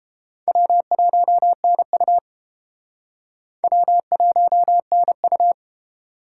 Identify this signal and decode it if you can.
Morse code, heard as a keyed tone, about 32 wpm, 700 Hz. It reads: W1NU W1NU